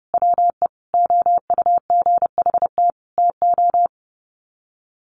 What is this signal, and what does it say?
Morse code, heard as a keyed tone, about 30 wpm, 700 Hz. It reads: WE OUGHT TO
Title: Sets of 5 Words Encore 30wpm